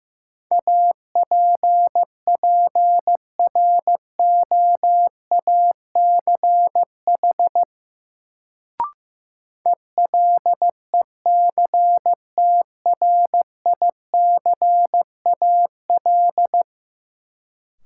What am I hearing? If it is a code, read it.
Morse code, heard as a keyed tone, about 15 wpm, 700 Hz. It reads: APPROACH ELECTRICAL
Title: Sentences from Top 1000 Words - Review Part 6 - New Word Drill - 15wpm